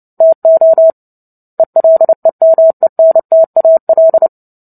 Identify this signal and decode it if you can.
Morse code, heard as a keyed tone, about 30 wpm, 650 Hz. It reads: TO ELEMENTAL
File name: Book Two / 03-Chapter 3 - 29 WPM